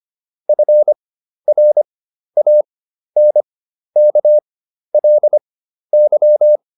Morse code, heard as a keyed tone, about 25 wpm, spaced as for 12 wpm, 600 Hz. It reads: FRANKLY